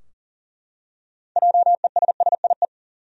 Morse code, heard as a keyed tone, about 40 wpm, 700 Hz. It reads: JESSIE
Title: USA Common Names - 40wpm